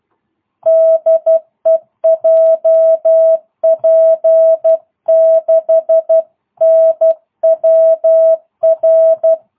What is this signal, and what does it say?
Morse code, heard as a keyed tone, about 12 wpm, 650 Hz. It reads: DE JP6NWR